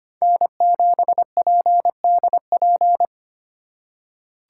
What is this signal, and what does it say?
Morse code, heard as a keyed tone, about 25 wpm, 700 Hz. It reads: N7PDP
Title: USA Call Signs - 5 Characters - 3x Character Spacing - 25wpm